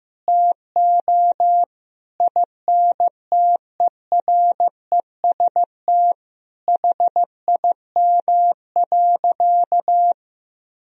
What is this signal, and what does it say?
Morse code, heard as a keyed tone, about 15 wpm, 700 Hz. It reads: TO INTEREST HIM.
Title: Sentences from Top 1000 Words - Review Part 2 - 15wpm